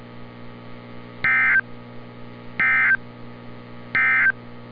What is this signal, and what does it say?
Emergency Alert System (SAME) tones, heard as harsh buzzing bursts. It is heard from 1.1 to 4.4 s.
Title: NOAA Weather Radio - Altoona, PA